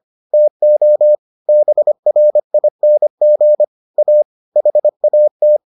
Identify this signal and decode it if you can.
Morse code, heard as a keyed tone, about 25 wpm, 600 Hz. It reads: TO BRING A HAT